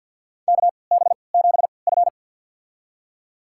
Morse code, heard as a keyed tone, about 51 wpm, spaced as for 28 wpm, 700 Hz. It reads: KB7F
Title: USA Call Signs - 3x Character Spacing - 50wpm